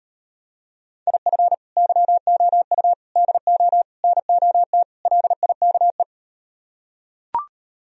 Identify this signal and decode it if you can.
Morse code, heard as a keyed tone, about 38 wpm, 700 Hz. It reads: IF YOU DO NOT LIKE